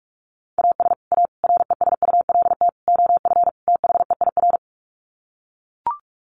Morse code, heard as a keyed tone, about 44 wpm, 700 Hz. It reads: AS A RESULT OF THEIR
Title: Sets of 5 Words Encore2 45wpm